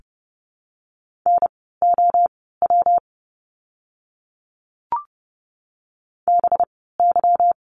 Morse code, heard as a keyed tone, about 30 wpm, spaced as for 17 wpm, 700 Hz. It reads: NOW BY
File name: course / 30:10wpm / 230 Lesson 37f - Letter x - Words 30wpm